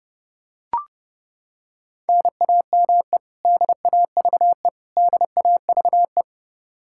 Morse code, heard as a keyed tone, about 32 wpm, 700 Hz. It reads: NAME DAVE DAVE
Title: Practice QSO - Exchange 1 - Beginning - 30wpm